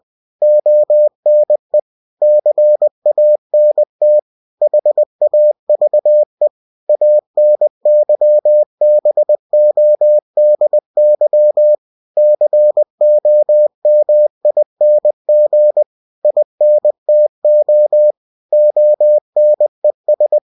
Morse code, heard as a keyed tone, about 20 wpm, 600 Hz. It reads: ONE CANT HAVE ANYBODY COMING INTO ONES
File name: cw-books / winnie-the-pooh / winnie-20wpm-page-0015